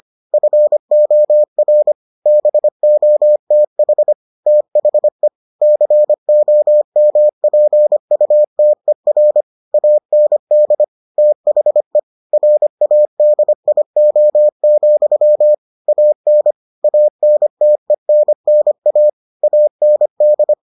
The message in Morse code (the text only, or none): FOR BOTH THE COMPUTER AND THE RADIO, AN ANTENNA AND